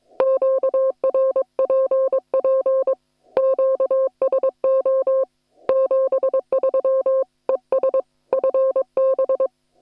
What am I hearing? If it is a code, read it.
Morse code, heard as a keyed tone, about 22 wpm, 530 Hz. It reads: QRPP QSO 73 ES FB
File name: attachments / hf 1 001